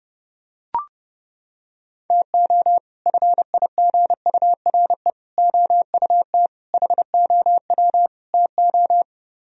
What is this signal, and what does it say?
Morse code, heard as a keyed tone, about 28 wpm, 700 Hz. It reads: TO FIGURE OUT HOW TO